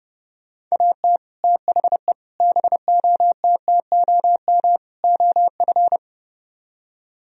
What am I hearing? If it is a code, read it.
Morse code, heard as a keyed tone, about 36 wpm, 700 Hz. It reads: AT THE BOTTOM OF
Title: Speed-Racing - Sets of 4 Words 30wpm